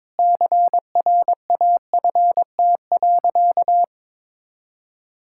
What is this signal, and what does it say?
Morse code, heard as a keyed tone, about 22 wpm, 700 Hz. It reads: CRAFT.
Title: Sentences from Top 1000 Words - Review Part 6 - 22wpm